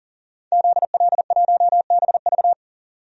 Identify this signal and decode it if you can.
Morse code, heard as a keyed tone, about 40 wpm, 700 Hz. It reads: ZL1BV
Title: DXCC-100 Call Signs - 5 Characters - 40wpm